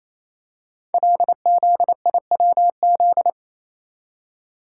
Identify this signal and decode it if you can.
Morse code, heard as a keyed tone, about 27 wpm, 700 Hz. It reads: LZIWZ
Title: Code Groups - 5 Letters - 28wpm